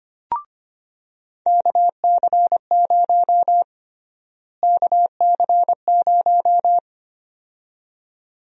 Morse code, heard as a keyed tone, about 25 wpm, 700 Hz. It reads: KC0 KC0
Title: USA Call Sign Prefixes - Repeated 2x - 25wpm